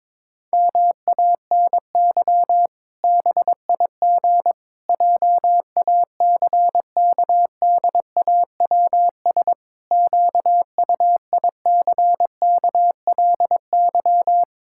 Morse code, heard as a keyed tone, about 22 wpm, 700 Hz. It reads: MANY BIG JACKDAWS QUICKLY
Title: Pangram Challenge v2 - 22wpm